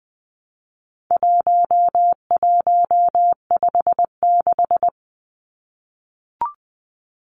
Morse code, heard as a keyed tone, about 20 wpm, 700 Hz. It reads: 1156
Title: Numbers - 4-Digit 20wpm